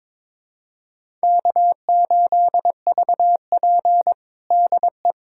Morse code, heard as a keyed tone, about 22 wpm, 700 Hz. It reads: K8VP DE
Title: USA Callsign DE Callsign - 4 Characters - 22wpm